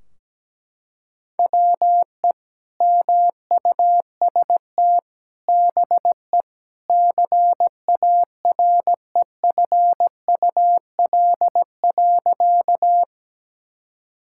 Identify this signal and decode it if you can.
Morse code, heard as a keyed tone, about 17 wpm, 700 Hz. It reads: WE MUST BE CAREFUL.
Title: Sentences from Top 1600 Words 17wpm